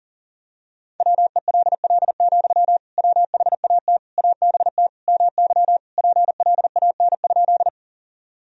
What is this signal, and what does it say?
Morse code, heard as a keyed tone, about 40 wpm, 700 Hz. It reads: WELL, WHAT ABT MY PLAN?
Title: Sentences from Top 500 Words and Abbreviations 40wpm